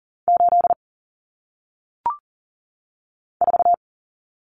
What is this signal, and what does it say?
Morse code, heard as a keyed tone, about 39 wpm, 700 Hz. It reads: 8 4
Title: Numbers - 1-Digit 40wpm